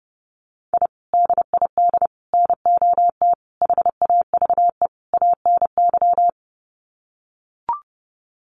Morse code, heard as a keyed tone, about 30 wpm, 700 Hz. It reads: I DID NOT HAVE ANY
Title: Sets of 5 Words Encore 30wpm